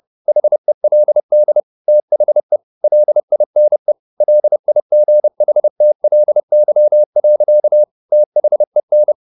Morse code, heard as a keyed tone, about 30 wpm, 600 Hz. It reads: HELD THE LINE LIGHTLY. THEN
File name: the old man and the sea / the old man and the sea 1 30wpm